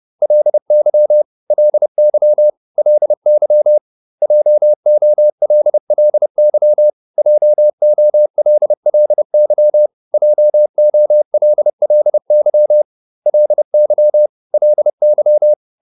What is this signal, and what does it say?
Morse code, heard as a keyed tone, about 30 wpm, 600 Hz. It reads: LY LY LY JOLLY JOLLY JOLLY LY LY